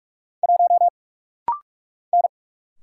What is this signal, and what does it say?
Morse code, heard as a keyed tone, about 48 wpm, 700 Hz. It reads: 1 N
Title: Single Letter-Number - Rapid-Fire 45wpm